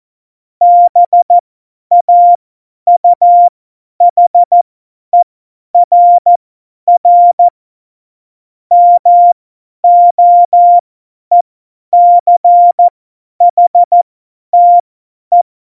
Morse code, heard as a keyed tone, about 14 wpm, spaced as for 10 wpm, 700 Hz. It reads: BAUHERR MOECHTE
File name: Morsekurs / Witze / Witz 22. Oktober 2019, 50 F70